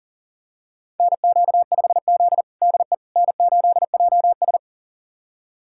Morse code, heard as a keyed tone, about 40 wpm, 700 Hz. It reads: NQ5Z DE N8JS
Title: USA Callsign DE Callsign - 4 Characters - 40wpm